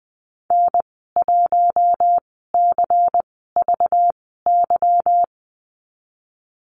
Morse code, heard as a keyed tone, about 20 wpm, spaced as for 15 wpm, 700 Hz. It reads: N1CVY
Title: 213 Lesson 34g - Number 8 - Call Signs 20wpm